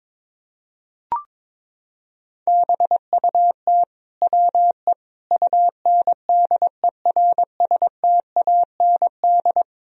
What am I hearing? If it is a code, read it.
Morse code, heard as a keyed tone, about 22 wpm, 700 Hz. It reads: BUT WE UNDERSTAND
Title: Sentences from Top 1000 Words - Review Part 1 - 22wpm